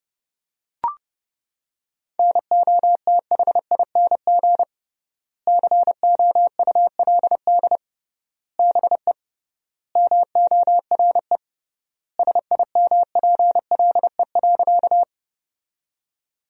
Morse code, heard as a keyed tone, about 30 wpm, 700 Hz. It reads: NOTHING COULD BE MORE SIMPLE.
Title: 3x Word-Spacing - Sentences from Top 500 Words - 30wpm